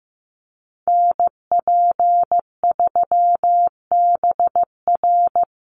Morse code, heard as a keyed tone, about 15 wpm, 700 Hz. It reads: NP3BR
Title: USA Call Signs Repeated 2x 15wpm